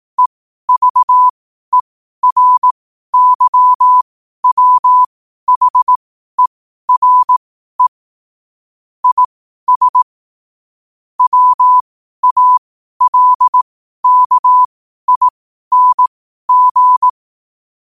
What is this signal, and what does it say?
Morse code, heard as a keyed tone, about 18 wpm, spaced as for 12 wpm, 1000 Hz. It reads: EVERYWHERE IS WALKING
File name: qotdfiles / 12wpm